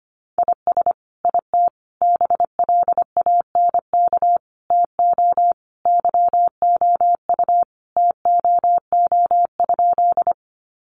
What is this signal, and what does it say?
Morse code, heard as a keyed tone, about 25 wpm, 700 Hz. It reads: IS IT BLANK TO YOU TOO?
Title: Sentences from Top 1500 Words 25wpm